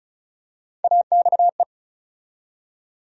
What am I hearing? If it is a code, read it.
Morse code, heard as a keyed tone, about 34 wpm, 700 Hz. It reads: AXE
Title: USA Call Sign Postfixes - 3 Letters 35wpm